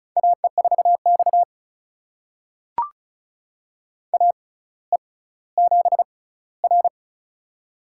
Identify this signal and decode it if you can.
Morse code, heard as a keyed tone, about 36 wpm, spaced as for 18 wpm, 700 Hz. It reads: AE4X AE7R
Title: USA Call Signs - 4 Characters - 6x Character Spacing - 35wpm